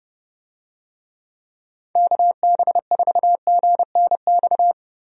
Morse code, heard as a keyed tone, about 30 wpm, 700 Hz. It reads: KB4GNX
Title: USA Call Signs - 6 Characters - Repeated 2x - 30wpm